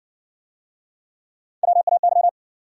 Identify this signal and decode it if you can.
Morse code, heard as a keyed tone, about 59 wpm, 700 Hz. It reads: WAY